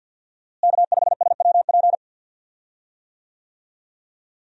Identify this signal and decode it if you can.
Morse code, heard as a keyed tone, about 53 wpm, 700 Hz. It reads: K5SWP
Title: USA Call Signs - 5 Characters - Repeated 2x - 50wpm